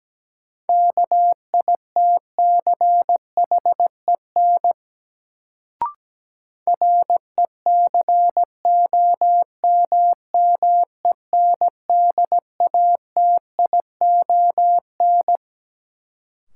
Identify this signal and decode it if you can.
Morse code, heard as a keyed tone, about 17 wpm, 700 Hz. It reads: KITCHEN RECOMMENDATION
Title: Sentences from Top 1400 Words - New Word Drill - 17wpm